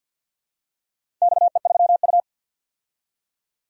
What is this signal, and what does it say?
Morse code, heard as a keyed tone, about 47 wpm, 700 Hz. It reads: XE3U
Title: DXCC-100 Call Signs - 4 Characters - 50wpm